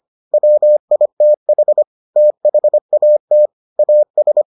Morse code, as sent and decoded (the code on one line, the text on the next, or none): .-- .. - .... / - .... .- - / .- ...
WITH THAT AS